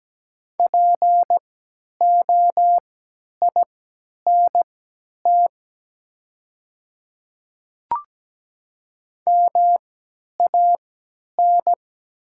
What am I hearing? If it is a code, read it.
Morse code, heard as a keyed tone, about 17 wpm, spaced as for 10 wpm, 700 Hz. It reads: POINT MAN
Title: 3x Character Spacing - Top 200 Words - 17wpm